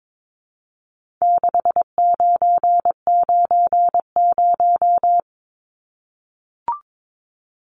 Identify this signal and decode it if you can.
Morse code, heard as a keyed tone, about 22 wpm, 700 Hz. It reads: 6990
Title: Numbers - 4-Digit 22wpm